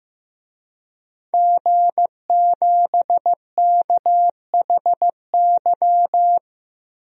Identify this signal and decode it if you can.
Morse code, heard as a keyed tone, about 15 wpm, 700 Hz. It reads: G7KHY
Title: DXCC-100 Call Signs - 5 Characters - 15wpm